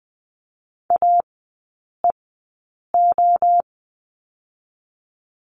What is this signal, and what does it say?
Morse code, heard as a keyed tone, about 20 wpm, 700 Hz. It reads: A E O
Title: 023 Lesson 5e - Letter o - Three Characters 20wpm